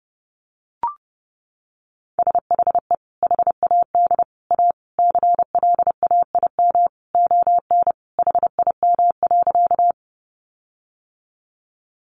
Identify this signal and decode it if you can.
Morse code, heard as a keyed tone, about 30 wpm, 700 Hz. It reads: SHE HAD A CLAIM ON HIM.
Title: Sentences from Top 1500 Words 30wpm